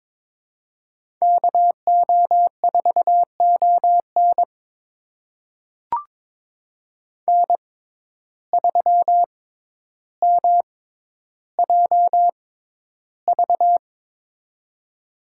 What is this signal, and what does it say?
Morse code, heard as a keyed tone, about 22 wpm, spaced as for 11 wpm, 700 Hz. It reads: KO4ON N3MJV
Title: USA Call Signs - 5 Characters - 6x Character Spacing - 22wpm